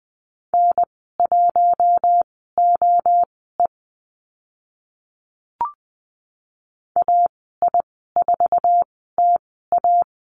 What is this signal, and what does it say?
Morse code, heard as a keyed tone, about 20 wpm, spaced as for 14 wpm, 700 Hz. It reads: N1OE AI4TA
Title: 082 Lesson 14g - Number 2 - Call Signs 20wpm